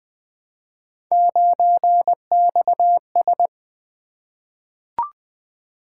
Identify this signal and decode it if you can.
Morse code, heard as a keyed tone, about 20 wpm, 700 Hz. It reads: 9XS
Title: Three Letter-Number Combinations - 20wpm